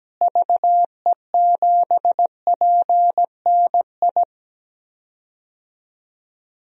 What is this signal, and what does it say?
Morse code, heard as a keyed tone, about 17 wpm, 700 Hz. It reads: VE7PNI
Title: DXCC-100 Call Signs - 6 Characters - Repeated 3x - 17wpm